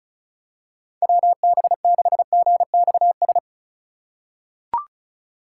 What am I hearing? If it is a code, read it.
Morse code, heard as a keyed tone, about 36 wpm, 700 Hz. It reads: WB6GXS
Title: USA Call Signs - 6 Characters - 3x Character Spacing - 35wpm